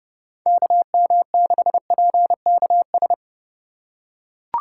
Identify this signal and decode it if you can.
Morse code, heard as a keyed tone, about 29 wpm, 700 Hz. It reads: KM6PKS